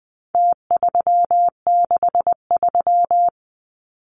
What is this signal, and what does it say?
Morse code, heard as a keyed tone, about 20 wpm, 700 Hz. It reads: T363
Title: Numbers - Contest Style - 4-Digit 20wpm